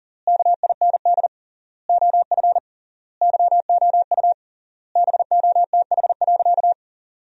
Morse code, heard as a keyed tone, about 37 wpm, 700 Hz. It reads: KIND OF YOU BOTH.
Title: 3x Word-Spacing - Sentences from Top 300 Words - 40wpm